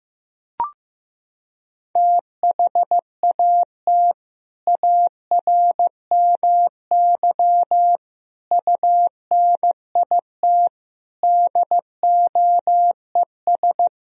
Morse code, heard as a keyed tone, about 15 wpm, 700 Hz. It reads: THAT ARMY UNIT DOES
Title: Sentences from Top 1000 Words - Review Part 2 - 15wpm